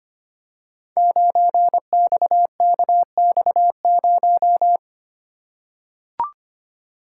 Morse code, heard as a keyed tone, about 25 wpm, 700 Hz. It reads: 9XKX0
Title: Code Groups - 5 Letter:Numbers - 25wpm